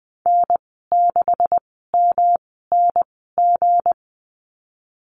Morse code, heard as a keyed tone, about 20 wpm, spaced as for 15 wpm, 700 Hz. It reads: N6MNG
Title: 179 Lesson 29g - Slash - Call Signs 20wpm